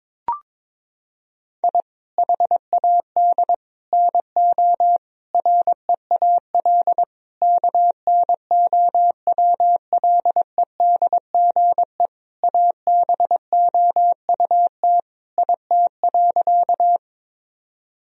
Morse code, heard as a keyed tone, about 22 wpm, 700 Hz. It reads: I HAD NO REAL KNOWLEDGE ABOUT IT.
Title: Sentences from Top 1000 Words - Review Part 3 - 22wpm